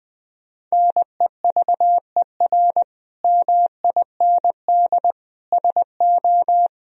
Morse code, heard as a keyed tone, about 20 wpm, 700 Hz. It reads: NEVER MIND SO
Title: Sentences from Top 1000 Words - Review Part 1 - 20wpm